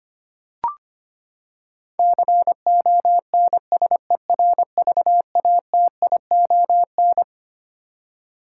Morse code, heard as a keyed tone, about 25 wpm, 700 Hz. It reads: CONSERVATION
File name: Twelve Letter Words - 25wpm